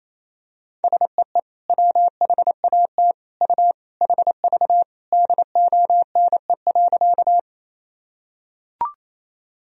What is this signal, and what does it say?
Morse code, heard as a keyed tone, about 28 wpm, 700 Hz. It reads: SEE WHAT U HV DONE.
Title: Sentences from Top 500 Words and Abbreviations 28wpm